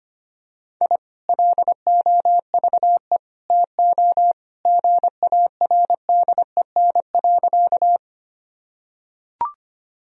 Morse code, heard as a keyed tone, about 25 wpm, 700 Hz. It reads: I LOVE TO GARDEN.
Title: Sentences from Top 1000 Words - Review Part 4 - 25wpm